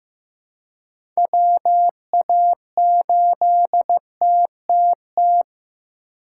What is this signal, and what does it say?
Morse code, heard as a keyed tone, about 15 wpm, 700 Hz. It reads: WA8TTT
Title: USA Call Signs - 6 Characters - 8x Character Spacing - 15wpm